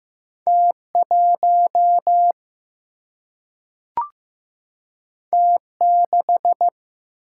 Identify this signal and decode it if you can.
Morse code, heard as a keyed tone, about 15 wpm, 700 Hz. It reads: T1 T6
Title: Single-Letter Single-Number 15wpm